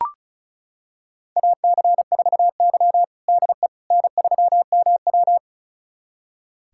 Morse code, heard as a keyed tone, about 36 wpm, 700 Hz. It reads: AC4Y DE N3MW
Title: USA Callsign DE Callsign - 4 Characters - 35wpm